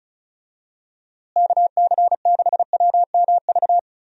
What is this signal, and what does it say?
Morse code, heard as a keyed tone, about 36 wpm, 700 Hz. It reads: KC6WMV
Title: USA Call Signs - 6 Characters - Repeated 3x - 35wpm